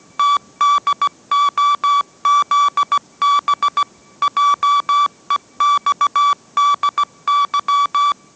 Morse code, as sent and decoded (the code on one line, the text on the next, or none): - -.. --- --.. -... / .--- . -..- -.. -.--
TDOZB JEXDY